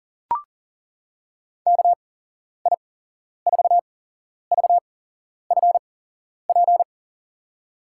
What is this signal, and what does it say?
Morse code, heard as a keyed tone, about 40 wpm, spaced as for 11 wpm, 700 Hz. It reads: KI4VFP
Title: USA Call Signs - 6 Characters - 8x Character Spacing - 40wpm